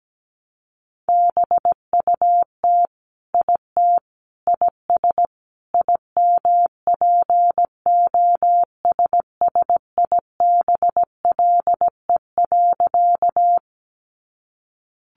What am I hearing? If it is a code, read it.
Morse code, heard as a keyed tone, about 17 wpm, 700 Hz. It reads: BUT IT IS IMPOSSIBLE.
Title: Sentences from Top 1100 Words 17wpm